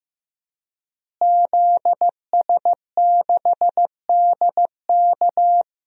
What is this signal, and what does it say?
Morse code, heard as a keyed tone, about 15 wpm, 700 Hz. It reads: ZS6DK
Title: DXCC-100 Call Signs - 5 Characters - Repeated 2x - 15wpm